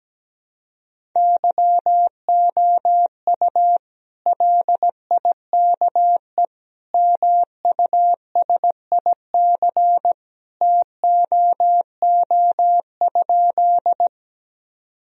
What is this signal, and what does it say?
Morse code, heard as a keyed tone, about 17 wpm, 700 Hz. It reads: YOU LIKE MUSIC TOO?
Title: Sentences from Top 1000 Words - Review Part 1 - 17wpm